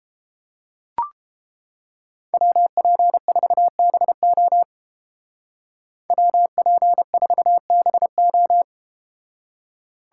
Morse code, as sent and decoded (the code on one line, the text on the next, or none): .-- .--. ....- -... --- / .-- .--. ....- -... ---
WP4BO WP4BO